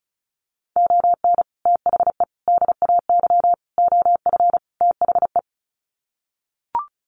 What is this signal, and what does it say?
Morse code, heard as a keyed tone, about 34 wpm, 700 Hz. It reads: ON THE DAY OF THE
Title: Sets of 5 Words Encore 35wpm